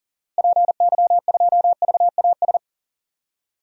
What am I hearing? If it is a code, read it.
Morse code, heard as a keyed tone, about 40 wpm, 700 Hz. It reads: PY2VAS
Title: DXCC-100 Call Signs - 6 Characters - Repeated 2x - 40wpm